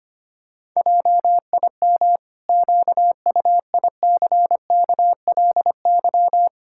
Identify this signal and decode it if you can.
Morse code, heard as a keyed tone, about 25 wpm, 700 Hz. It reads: JIM QUICKLY
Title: Pangram Challenge v2 - 25wpm